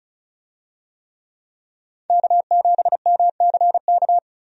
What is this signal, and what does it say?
Morse code, heard as a keyed tone, about 36 wpm, 700 Hz. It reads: K7MCK
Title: USA Call Signs - 5 Characters - 35wpm